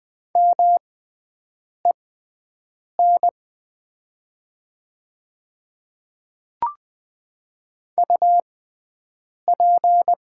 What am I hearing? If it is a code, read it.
Morse code, heard as a keyed tone, about 20 wpm, spaced as for 7 wpm, 700 Hz. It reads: MEN UP
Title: 6x Character Spacing - Top 300 Words - 20wpm